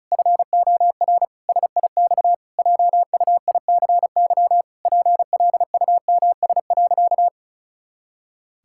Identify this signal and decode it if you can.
Morse code, heard as a keyed tone, about 36 wpm, 700 Hz. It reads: FOR SIX JUICY PLUMS.
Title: Pangram Challenge v2 - 35wpm